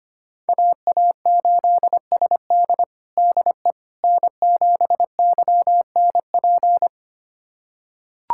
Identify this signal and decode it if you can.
Morse code, heard as a keyed tone, about 25 wpm, 700 Hz. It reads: AA8SD DE N7YNP
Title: USA Callsign DE Callsign - 5 Characters - 25wpm